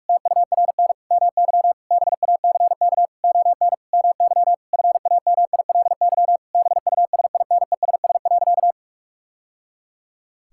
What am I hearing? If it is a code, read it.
Morse code, heard as a keyed tone, about 41 wpm, 700 Hz. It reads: TURN MY BACK ON MY FAMILY BUSINESS.
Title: Sentences from Top 1000 Words - Review Part 6 - 45wpm